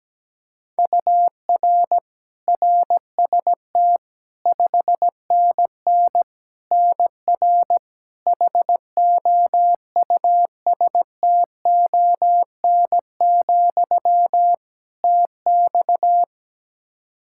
Morse code, heard as a keyed tone, about 17 wpm, 700 Hz. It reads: UR RST 5NN NR HOUSTON, TX
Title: K1USN SST QSO Practice 17wpm